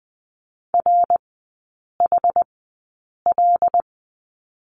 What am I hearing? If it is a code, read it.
Morse code, heard as a keyed tone, about 20 wpm, spaced as for 8 wpm, 700 Hz. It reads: RHL
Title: 121 Lesson 20e - Number 3 - Three Characters 20wpm